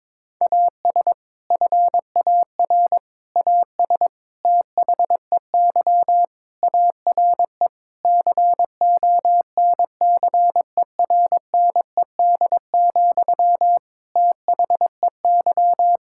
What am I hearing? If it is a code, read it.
Morse code, heard as a keyed tone, about 22 wpm, 700 Hz. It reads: AS FAR AS THEY ARE CONCERNED, THEY
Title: Sentences from Top 1000 Words - Review Part 6 - 22wpm